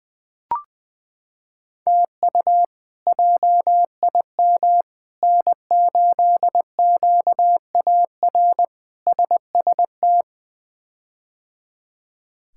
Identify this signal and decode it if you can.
Morse code, heard as a keyed tone, about 20 wpm, 700 Hz. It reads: TU JIM N8QAR SST
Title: K1USN SST QSO Practice - 20wpm